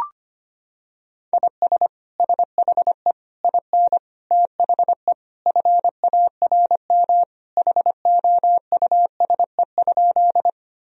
Morse code, heard as a keyed tone, about 25 wpm, 700 Hz. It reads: IS SHE IN THE FARM HOUSE?
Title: Sentences from Top 1000 Words - Review Part 2 - 25wpm